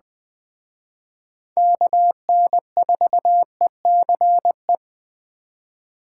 Morse code, heard as a keyed tone, about 20 wpm, 700 Hz. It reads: KN4ECE
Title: Speed Racing - USA 2x3 Call Signs 17wpm